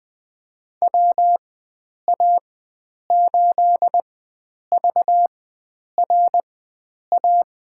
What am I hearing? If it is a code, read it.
Morse code, heard as a keyed tone, about 20 wpm, spaced as for 9 wpm, 700 Hz. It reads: WA8VRA
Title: USA Call Signs - 6 Characters - 4x Character Spacing - 20wpm